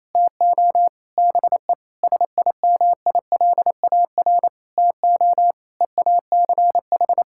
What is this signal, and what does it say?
Morse code, heard as a keyed tone, about 28 wpm, 700 Hz. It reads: TO BE SIMILAR TO EACH
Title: Sentences from Top 1000 Words - Review Part 5 - 28wpm